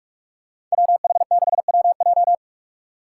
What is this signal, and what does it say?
Morse code, heard as a keyed tone, about 48 wpm, 700 Hz. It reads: WH6WJ
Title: USA Call Signs - 5 Characters - 2x Character Spacing - 45wpm